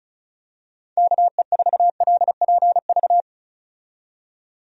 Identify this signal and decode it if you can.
Morse code, heard as a keyed tone, about 36 wpm, 700 Hz. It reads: KE4LPV
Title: USA Call Signs - 6 Characters - 35wpm